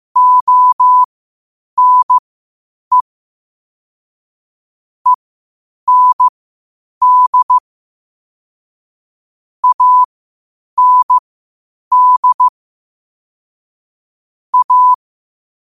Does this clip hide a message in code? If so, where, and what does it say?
Yes: Morse code from 0.2 to 14.9 s, heard as a keyed tone, about 15 wpm, spaced as for 8 wpm, 1000 Hz. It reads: ONE END AND A